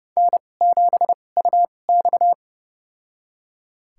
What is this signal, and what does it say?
Morse code, heard as a keyed tone, about 31 wpm, spaced as for 22 wpm, 700 Hz. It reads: N7UX
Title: USA Call Signs - 2x Character Spacing - 30wpm